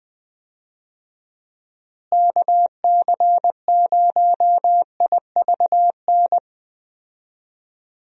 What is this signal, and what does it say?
Morse code, heard as a keyed tone, about 20 wpm, 700 Hz. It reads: KC0IVN
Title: USA Call Signs - 6 Characters - Repeated 2x - 20wpm